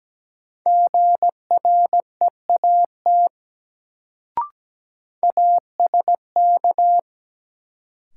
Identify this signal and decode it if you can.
Morse code, heard as a keyed tone, about 17 wpm, 700 Hz. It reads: GREAT ASK
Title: Sentences from Top 200 Words - New Word Drill - 17wpm